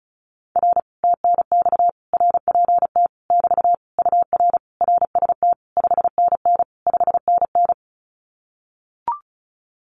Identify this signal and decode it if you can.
Morse code, heard as a keyed tone, about 36 wpm, 700 Hz. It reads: R TNX RPT <BT> UR RST 5NN 5NN
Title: Common QSO Phrases Encore 35wpm